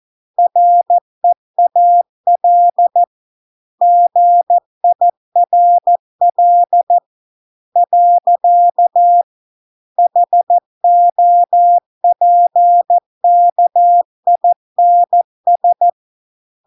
Morse code, heard as a keyed tone, about 14 wpm, 700 Hz. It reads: REAL GIRL . HOPKINS